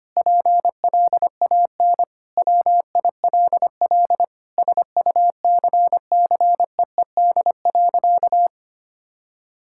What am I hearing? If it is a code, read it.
Morse code, heard as a keyed tone, about 25 wpm, 700 Hz. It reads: PLAN WILL SUCCEED.
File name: practice / 25wpm / Sentences with 9-Word Phrases - 25wpm